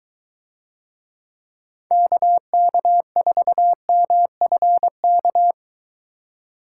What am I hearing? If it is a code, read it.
Morse code, heard as a keyed tone, about 22 wpm, 700 Hz. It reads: KK4MFK